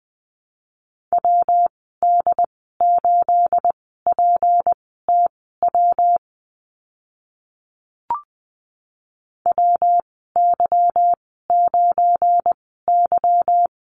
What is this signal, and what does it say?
Morse code, heard as a keyed tone, about 20 wpm, spaced as for 15 wpm, 700 Hz. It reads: WD8PTW WY9Y
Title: 213 Lesson 34g - Number 8 - Call Signs 20wpm